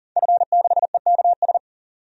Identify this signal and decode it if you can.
Morse code, heard as a keyed tone, about 38 wpm, 700 Hz. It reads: F6EKS
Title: DXCC-100 Call Signs - 5 Characters - 40wpm